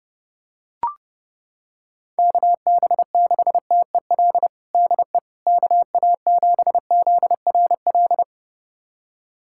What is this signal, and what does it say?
Morse code, heard as a keyed tone, about 30 wpm, 700 Hz. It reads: KB6TEL DE KA7ZRL
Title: USA Callsign DE Callsign 30wpm